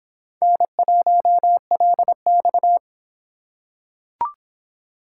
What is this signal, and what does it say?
Morse code, heard as a keyed tone, about 26 wpm, 700 Hz. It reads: N1LX